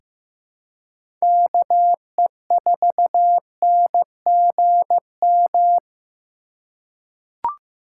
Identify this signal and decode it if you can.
Morse code, heard as a keyed tone, about 15 wpm, 700 Hz. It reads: KE4NGM
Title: USA Call Signs - 6 Characters - 3x Character Spacing - 15wpm